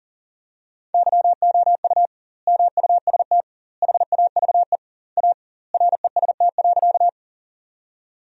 Morse code, heard as a keyed tone, about 37 wpm, 700 Hz. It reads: YOU MUST HAVE A REST.
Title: 2x Word-Spacing - Sentences from Top 500 Words - 40wpm